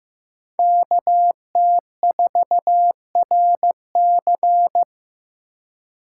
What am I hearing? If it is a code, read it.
Morse code, heard as a keyed tone, about 18 wpm, 700 Hz. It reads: KT4RC
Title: Speed Racing - USA Call Signs 15wpm